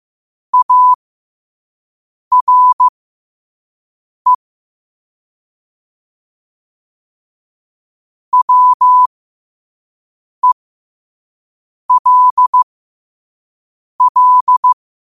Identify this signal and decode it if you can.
Morse code, heard as a keyed tone, about 15 wpm, spaced as for 5 wpm, 1000 Hz. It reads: ARE WELL
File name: qotdfiles / 5wpm